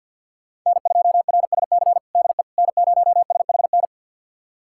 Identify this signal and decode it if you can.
Morse code, heard as a keyed tone, about 48 wpm, 700 Hz. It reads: N2RSC DE N0SHN
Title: USA Callsign DE Callsign - 5 Characters - 50wpm